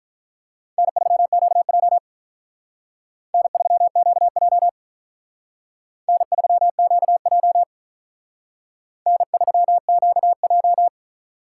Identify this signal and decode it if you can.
Morse code, heard as a keyed tone, about 43 wpm, 700 Hz. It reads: N3QJ N3QJ N3QJ N3QJ
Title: Speed Racing - USA 1x2 Call Signs 35wpm